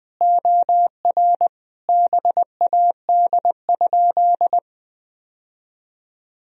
Morse code, heard as a keyed tone, about 20 wpm, 700 Hz. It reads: OR BAD?
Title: Sentences from Top 1000 Words - Review Part 1 - 20wpm